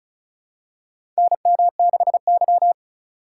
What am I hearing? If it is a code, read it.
Morse code, heard as a keyed tone, about 36 wpm, 700 Hz. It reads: NM6Y
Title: USA Call Signs - 4 Characters - 8x Character Spacing - 35wpm